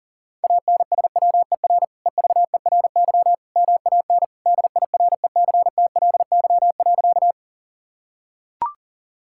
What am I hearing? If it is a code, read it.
Morse code, heard as a keyed tone, about 40 wpm, 700 Hz. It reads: ANSWER EVERY MAN DIRECTLY.
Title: Sentences from Top 1000 Words - Review Part 5 - 40wpm